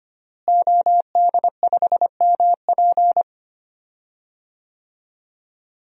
Morse code, heard as a keyed tone, about 25 wpm, 700 Hz. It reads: OD5MP
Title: DXCC-100 Call Signs - 5 Characters - Repeated 2x - 25wpm